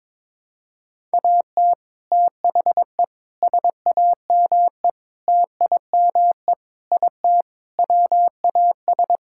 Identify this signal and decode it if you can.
Morse code, heard as a keyed tone, about 21 wpm, 700 Hz. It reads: AT THE SAME TIME IT WAS
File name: practice / 22wpm / Sentences with 3-Word Phrases - 22wpm